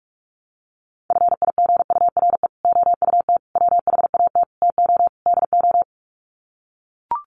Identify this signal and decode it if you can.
Morse code, heard as a keyed tone, about 44 wpm, 700 Hz. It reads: FIGURE OUT WHAT TO DO
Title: Sets of 5 Words Encore2 45wpm